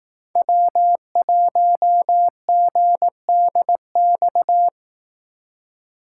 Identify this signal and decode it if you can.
Morse code, heard as a keyed tone, about 18 wpm, 700 Hz. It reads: W1GDX
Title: Speed Racing - USA 1x3 Call Signs 15wpm